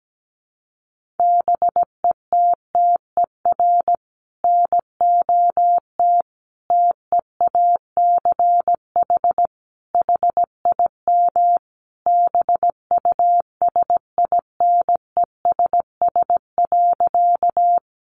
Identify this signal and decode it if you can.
Morse code, heard as a keyed tone, about 17 wpm, 700 Hz. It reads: BETTER NOT TEACH HIM BUSINESS.
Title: Sentences from Top 600 Words 17wpm